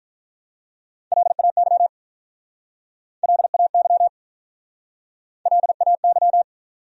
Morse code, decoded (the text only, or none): LAY LAY LAY